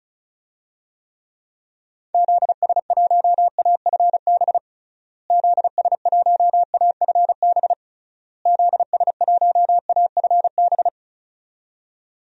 Morse code, heard as a keyed tone, about 36 wpm, 700 Hz. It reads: ZS1AFB ZS1AFB ZS1AFB
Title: DXCC-100 Call Signs - 6 Characters - Repeated 3x - 35wpm